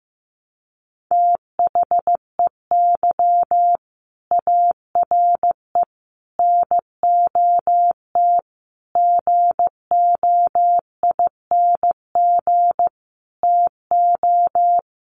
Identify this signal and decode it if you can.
Morse code, heard as a keyed tone, about 14 wpm, 700 Hz. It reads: THEY ARE NOT GOING TO
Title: Sets of 5 Words Encore2 15wpm